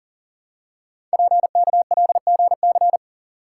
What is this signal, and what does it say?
Morse code, heard as a keyed tone, about 42 wpm, 700 Hz. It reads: PKLGC
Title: Code Groups - 5 Letter:Numbers - 40wpm